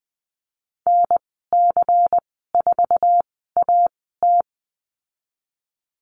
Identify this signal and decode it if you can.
Morse code, heard as a keyed tone, about 20 wpm, spaced as for 15 wpm, 700 Hz. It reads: NC4AT
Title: 147 Lesson 24g - Letter y - Call Signs 20wpm